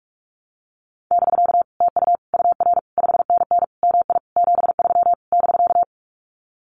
Morse code, heard as a keyed tone, about 44 wpm, 700 Hz. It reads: <BK> TU UR 5NN MI 73 <BK>
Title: Common QSO Phrases Encore 45wpm